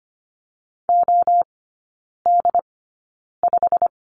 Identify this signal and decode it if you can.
Morse code, heard as a keyed tone, about 25 wpm, spaced as for 9 wpm, 690 Hz. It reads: OD5
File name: Lesson 15e - Number 5 - Three Characters 25wpm